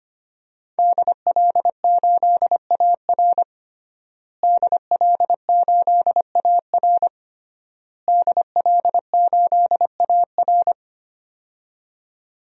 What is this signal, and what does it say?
Morse code, heard as a keyed tone, about 25 wpm, 700 Hz. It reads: DL8AR DL8AR DL8AR
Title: DXCC-100 Call Signs - 5 Characters - Repeated 3x - 25wpm